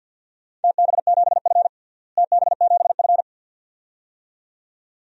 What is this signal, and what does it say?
Morse code, heard as a keyed tone, about 53 wpm, 700 Hz. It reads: TB7F TB7F
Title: DXCC-100 Call Signs - 4 Characters - Repeated 2x - 50wpm